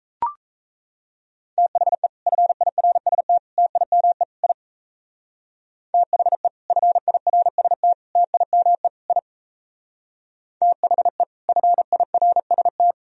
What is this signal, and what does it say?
Morse code, heard as a keyed tone, about 33 wpm, 700 Hz. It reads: THE FIRST TIME I THE FIRST TIME I THE FIRST
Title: Speed-Racing - Sets of 4 Words 28wpm